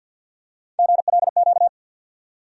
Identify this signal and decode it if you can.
Morse code, heard as a keyed tone, about 46 wpm, 700 Hz. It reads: GLQ
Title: USA Call Sign Postfixes - 3 Letters 50wpm